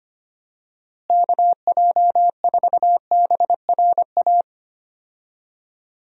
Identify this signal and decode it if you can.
Morse code, heard as a keyed tone, about 25 wpm, 700 Hz. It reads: KJ4BRA
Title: USA Call Signs - 6 Characters - Repeated 3x - 25wpm